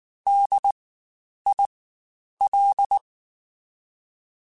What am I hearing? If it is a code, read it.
Morse code, heard as a keyed tone, about 19 wpm, spaced as for 9 wpm, 800 Hz. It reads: DIL
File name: Spaced Alphas / Spaced Alphas mp3s / Ds sp @18